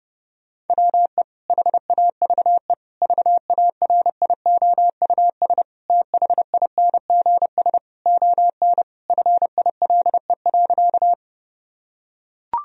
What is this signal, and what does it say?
Morse code, heard as a keyed tone, about 29 wpm, 700 Hz. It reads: WE HAVE VARIOUS THINGS ON FILE.
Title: Sentences from Top 1000 Words - Review Part 4 - 30wpm